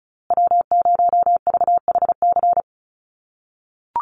Morse code, heard as a keyed tone, about 34 wpm, 700 Hz. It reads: W0VHC
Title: USA Call Signs Repeated 3x 35wpm